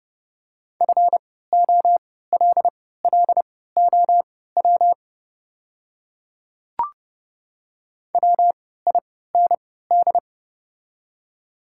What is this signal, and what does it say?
Morse code, heard as a keyed tone, about 31 wpm, spaced as for 17 wpm, 700 Hz. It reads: FOLLOW WIND
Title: 3x Character Spacing - Top 500 Words - 30wpm